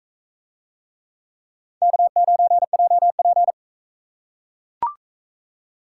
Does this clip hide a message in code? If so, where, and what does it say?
Yes: Morse code from 1.8 to 3.5 s, heard as a keyed tone, about 36 wpm, 700 Hz. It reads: K9JP